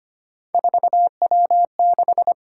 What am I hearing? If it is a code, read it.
Morse code, heard as a keyed tone, about 29 wpm, 700 Hz. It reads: 4W6
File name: Speed-Racing - Three Letter-Number Combinations - 25wpm